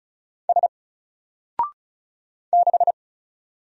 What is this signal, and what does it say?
Morse code, heard as a keyed tone, about 34 wpm, 700 Hz. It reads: S 6
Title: Single Letter-Number 35wpm